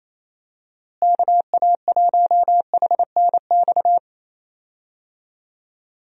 Morse code, heard as a keyed tone, about 28 wpm, 700 Hz. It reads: KA1HNX
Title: USA Call Signs - 6 Characters - Repeated 2x - 28wpm